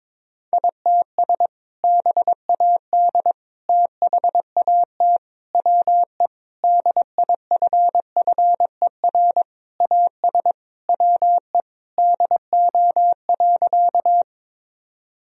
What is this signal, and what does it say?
Morse code, heard as a keyed tone, about 21 wpm, 700 Hz. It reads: ITS BAD THAT WE DIFFER AS WE DO.
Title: Sentences from Top 1000 Words - Review Part 1 - 22wpm